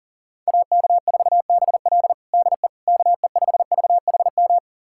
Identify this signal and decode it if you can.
Morse code, heard as a keyed tone, about 37 wpm, 700 Hz. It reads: AK4BL DE KE5VHM
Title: USA Callsign DE Callsign 40wpm